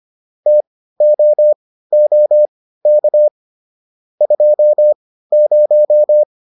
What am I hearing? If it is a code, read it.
Morse code, heard as a keyed tone, about 25 wpm, spaced as for 15 wpm, 600 Hz. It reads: TOOK 20